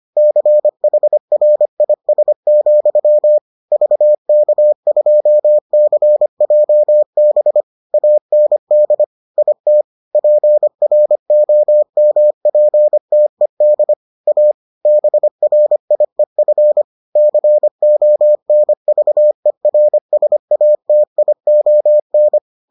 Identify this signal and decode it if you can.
Morse code, heard as a keyed tone, about 25 wpm, 600 Hz. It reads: CHRIS, VK2CJB AND IT PROMPTED A BRIEF CONVERSATION